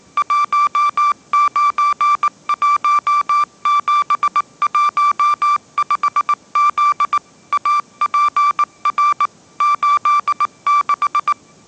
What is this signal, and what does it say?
Morse code, heard as a keyed tone, about 19 wpm, 1170 Hz. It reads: 191715Z APR 86
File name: mrx083 20wpm